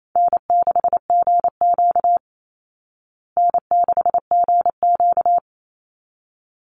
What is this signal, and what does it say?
Morse code, heard as a keyed tone, about 28 wpm, 700 Hz. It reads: N6GQ N6GQ
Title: Easy Call Signs Repeated 2x - 28wpm